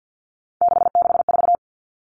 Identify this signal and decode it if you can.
Morse code, heard as a keyed tone, about 50 wpm, 700 Hz. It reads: 664
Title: Common Cut Numbers - 3-Digit 50wpm